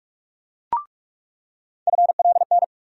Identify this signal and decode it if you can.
Morse code, heard as a keyed tone, about 47 wpm, 700 Hz. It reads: FLN
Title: USA Call Sign Postfixes - 3 Letters 45wpm